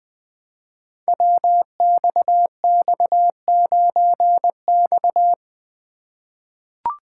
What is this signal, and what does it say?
Morse code, heard as a keyed tone, about 20 wpm, 700 Hz. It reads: WXX9X
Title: Code Groups - 5 Letter:Numbers - 20wpm